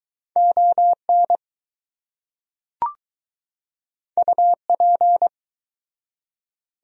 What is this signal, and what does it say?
Morse code, heard as a keyed tone, about 21 wpm, 700 Hz. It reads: ON UP